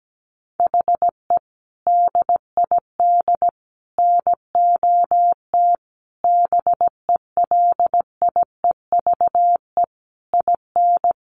Morse code, heard as a keyed tone, about 17 wpm, 700 Hz. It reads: HE DID NOT BELIEVE IN
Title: Sentences from Top 1300 Words 17wpm